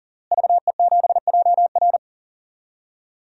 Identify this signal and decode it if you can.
Morse code, heard as a keyed tone, about 40 wpm, 700 Hz. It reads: VE7JR
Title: DXCC-100 Call Signs - 5 Characters - Repeated 2x - 40wpm